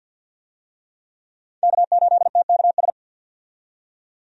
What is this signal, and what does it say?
Morse code, heard as a keyed tone, about 50 wpm, 700 Hz. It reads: K8TKS
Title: USA Call Signs - 5 Characters - 50wpm